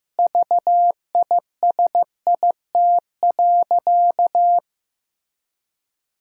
Morse code, heard as a keyed tone, about 15 wpm, 700 Hz. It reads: VISIT.
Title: Sentences from Top 1000 Words - Review Part 6 - 15wpm